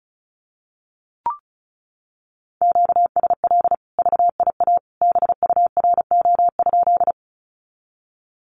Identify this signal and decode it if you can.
Morse code, heard as a keyed tone, about 34 wpm, 700 Hz. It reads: QSL VIA BURO?
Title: Common QSO Phrases 35wpm